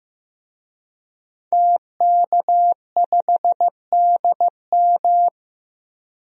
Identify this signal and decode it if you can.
Morse code, heard as a keyed tone, about 15 wpm, 700 Hz. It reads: TK5DM
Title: DXCC-100 Call Signs - 5 Characters - Repeated 2x - 15wpm